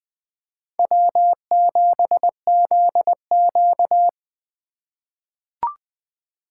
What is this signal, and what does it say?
Morse code, heard as a keyed tone, about 20 wpm, 700 Hz. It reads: W7ZQ
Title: USA Call Signs - 2x Character Spacing - 20wpm